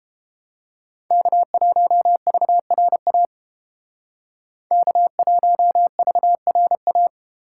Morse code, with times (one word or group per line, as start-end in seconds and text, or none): 1.1-3.3 K1VRA
4.7-7.1 K1VRA